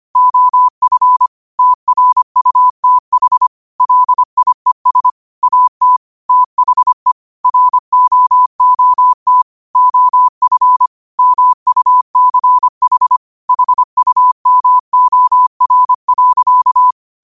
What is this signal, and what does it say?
Morse code, heard as a keyed tone, about 25 wpm, 1000 Hz. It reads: OF TRUTH LIES AT THE ROOT OF MUCH HUMOR.
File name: qotdfiles / 25wpm